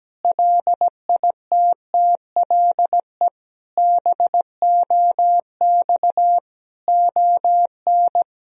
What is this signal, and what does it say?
Morse code, heard as a keyed tone, about 17 wpm, 700 Hz. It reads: LITTLE BOX ON
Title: Sentences from Top 1000 Words - Review Part 2 - 17wpm